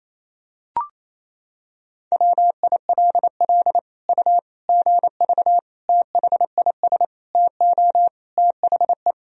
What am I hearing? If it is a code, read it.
Morse code, heard as a keyed tone, about 27 wpm, 700 Hz. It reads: WILL U GV THIS TO THE
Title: Sentences from Top 200 Words and Abbreviations 28wpm